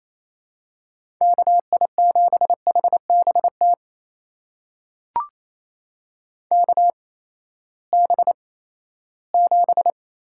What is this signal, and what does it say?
Morse code, heard as a keyed tone, about 28 wpm, spaced as for 16 wpm, 700 Hz. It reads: KI7HBT KB7
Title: USA Call Signs - 6 Characters - 8x Character Spacing - 28wpm